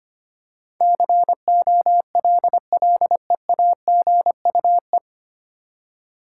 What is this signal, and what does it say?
Morse code, heard as a keyed tone, about 25 wpm, 700 Hz. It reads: COLLEAGUE
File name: Nine Letter Words - 25wpm